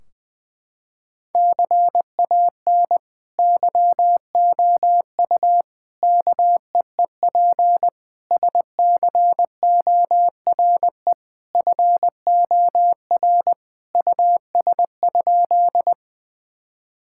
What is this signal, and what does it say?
Morse code, heard as a keyed tone, about 20 wpm, 700 Hz. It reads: CAN YOU KEEP SCORE FOR US?
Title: Sentences from Top 1600 Words 20wpm